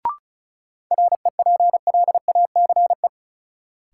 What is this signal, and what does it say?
Morse code, heard as a keyed tone, about 34 wpm, 700 Hz. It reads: REPLACE
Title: Sentences from Top 1400 Words - New Word Drill - 35wpm